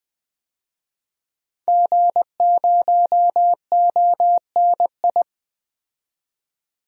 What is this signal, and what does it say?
Morse code, heard as a keyed tone, about 20 wpm, 700 Hz. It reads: G0ONI
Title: DXCC-100 Call Signs - 5 Characters - 20wpm